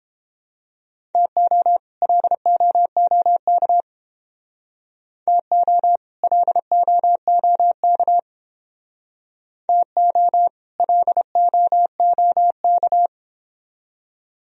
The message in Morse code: TO LOOK TO LOOK TO LOOK

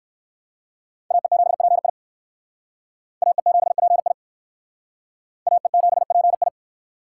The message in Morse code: AE7PI AE7PI AE7PI

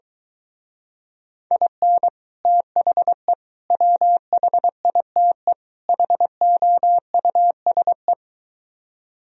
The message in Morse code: IN THE WHITE HOUSE

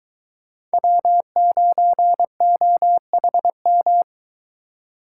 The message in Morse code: W9OHM

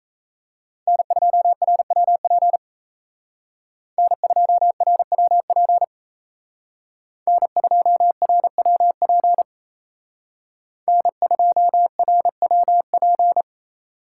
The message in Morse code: N2RWP N2RWP N2RWP N2RWP

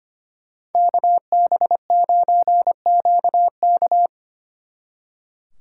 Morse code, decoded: KB9QK

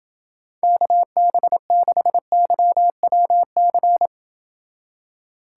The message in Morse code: KB6YWC

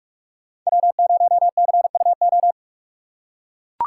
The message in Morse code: W0CUO

